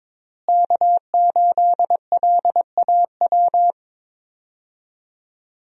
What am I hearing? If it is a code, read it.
Morse code, heard as a keyed tone, about 22 wpm, 700 Hz. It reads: K8LAW
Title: USA Call Signs - 5 Characters - Repeated 3x - 22wpm